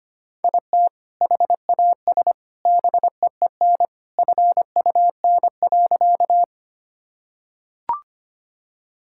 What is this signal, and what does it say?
Morse code, heard as a keyed tone, about 25 wpm, 700 Hz. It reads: IT HAS BEEN FUN.
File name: practice / 25wpm / Sentences with 2-Word Phrases - 25wpm